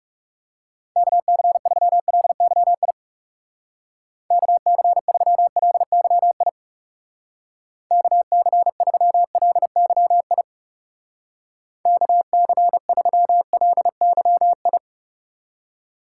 Morse code, heard as a keyed tone, about 36 wpm, 700 Hz. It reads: KC3LYI KC3LYI KC3LYI KC3LYI
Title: Speed Racing - USA 2x3 Call Signs 30wpm